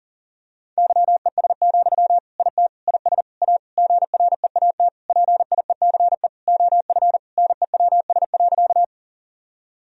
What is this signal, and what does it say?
Morse code, heard as a keyed tone, about 37 wpm, 700 Hz. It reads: YES, IT IS A GREAT PIECE OF NEWS.
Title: Sentences from Top 1000 Words - Review Part 2 - 40wpm